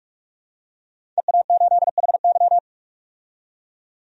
Morse code, heard as a keyed tone, about 43 wpm, 700 Hz. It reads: EA8HY
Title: DXCC-100 Call Signs - 5 Characters - 45wpm